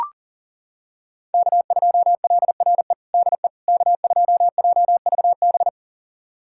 Morse code, heard as a keyed tone, about 40 wpm, 700 Hz. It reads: K2LRE DE K2JVB
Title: USA Callsign DE Callsign - 5 Characters - 40wpm